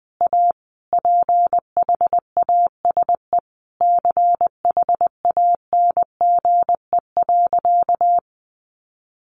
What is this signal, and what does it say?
Morse code, heard as a keyed tone, about 20 wpm, 700 Hz. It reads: A PHASE CHANGE.
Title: Sentences from Top 1200 Words 20wpm